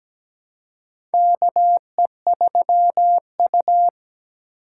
Morse code, heard as a keyed tone, about 17 wpm, 700 Hz. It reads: KE3U
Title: USA Call Signs - 3x Character Spacing - 17wpm